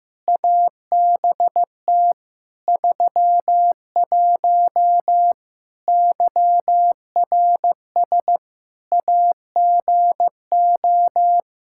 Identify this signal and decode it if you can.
Morse code, heard as a keyed tone, about 15 wpm, 700 Hz. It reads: ABT 31 YRS AGO
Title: Practice QSO - Exchange 3 - More Info - 15wpm